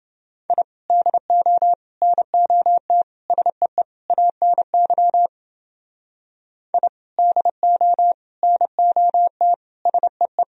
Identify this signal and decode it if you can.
Morse code, heard as a keyed tone, about 28 wpm, 700 Hz. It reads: I DO NOT SEE ANY I DO NOT SEE